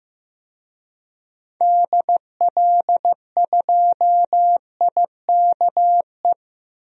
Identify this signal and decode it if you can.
Morse code, heard as a keyed tone, about 15 wpm, 700 Hz. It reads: DL2IKE
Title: DXCC-100 Call Signs - 6 Characters - 15wpm